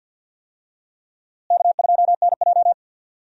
Morse code, heard as a keyed tone, about 48 wpm, 700 Hz. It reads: K2NJ
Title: USA Call Signs - 4 Characters - Repeated 2x - 50wpm